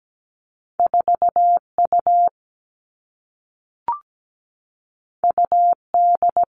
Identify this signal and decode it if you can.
Morse code, heard as a keyed tone, about 17 wpm, 700 Hz. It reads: 4 U U D
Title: Two Challenging Letter-Number Combinations 17wpm